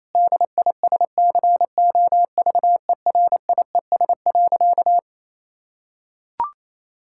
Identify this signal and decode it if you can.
Morse code, heard as a keyed tone, about 28 wpm, 700 Hz. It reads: DISCOVERIES.
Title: Pangram Challenge v2 - 28wpm